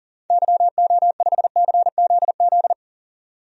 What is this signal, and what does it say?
Morse code, heard as a keyed tone, about 40 wpm, 700 Hz. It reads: YO5CZZ
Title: DXCC-100 Call Signs - 6 Characters - 40wpm